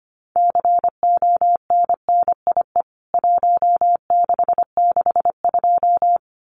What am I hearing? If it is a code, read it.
Morse code, heard as a keyed tone, about 25 wpm, 700 Hz. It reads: CONNIE 1662
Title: CWT Exchange 25wpm